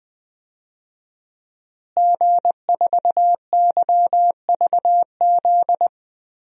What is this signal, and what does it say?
Morse code, heard as a keyed tone, about 21 wpm, 700 Hz. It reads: G4YVZ
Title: DXCC-100 Call Signs - 5 Characters - Repeated 2x - 20wpm